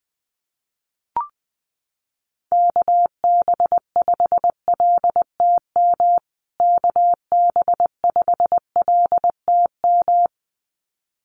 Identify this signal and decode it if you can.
Morse code, heard as a keyed tone, about 20 wpm, 700 Hz. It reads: KB5LTM KB5LTM
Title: USA Call Signs Repeated 2x 20wpm